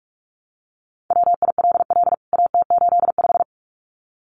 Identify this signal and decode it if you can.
Morse code, heard as a keyed tone, about 44 wpm, 700 Hz. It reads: WILL AT85
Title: CWT Exchange 45wpm